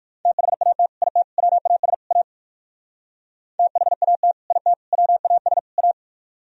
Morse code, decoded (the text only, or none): THAT IT WAS A THAT IT WAS A